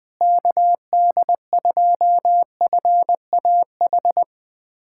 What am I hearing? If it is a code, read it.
Morse code, heard as a keyed tone, about 20 wpm, 700 Hz. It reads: KD2FAH
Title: USA Callsign DE Callsign 20wpm